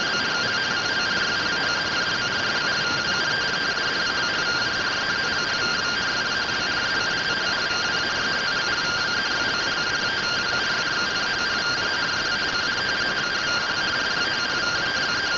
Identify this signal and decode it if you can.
Two-tone teleprinter signal, about 170 Hz shift.